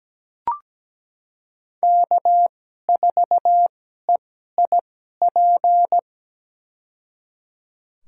Morse code, heard as a keyed tone, about 17 wpm, spaced as for 12 wpm, 700 Hz. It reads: K4EIP
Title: USA Call Signs - 5 Characters - 2x Character Spacing - 17wpm